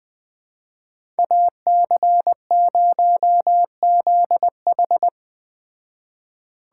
Morse code, heard as a keyed tone, about 20 wpm, 700 Hz. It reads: AC0ZH
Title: Speed Racing - USA 2x2 Call Signs 17wpm